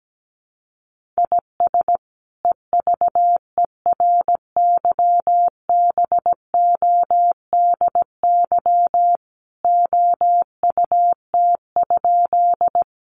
Morse code, heard as a keyed tone, about 17 wpm, 700 Hz. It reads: IS EVERYBODY OUT?
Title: Sentences from Top 1400 Words 17wpm